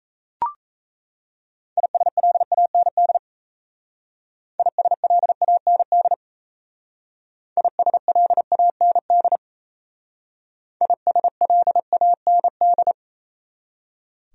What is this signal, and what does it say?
Morse code, heard as a keyed tone, about 34 wpm, 700 Hz. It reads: ISLAND ISLAND ISLAND ISLAND
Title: Speed-Racing Top 500 Words 28wpm